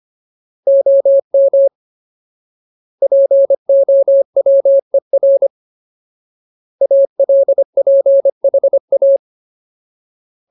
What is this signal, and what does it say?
Morse code, heard as a keyed tone, about 25 wpm, 550 Hz. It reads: OM POWER ALPHA
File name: Ultimate Word List / 25WPM Ultimate